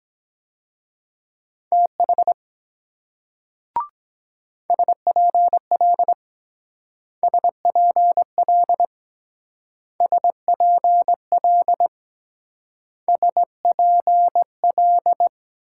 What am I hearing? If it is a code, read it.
Morse code, heard as a keyed tone, about 21 wpm, 700 Hz. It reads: TH SPL SPL SPL SPL
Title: Speed-Racing Multi-letter Phonemes - 17wpm